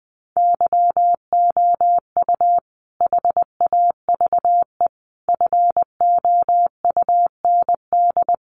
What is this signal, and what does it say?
Morse code, heard as a keyed tone, about 20 wpm, 700 Hz. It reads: YOU HAVE FOUND
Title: Sentences from Top 1500 Words 20wpm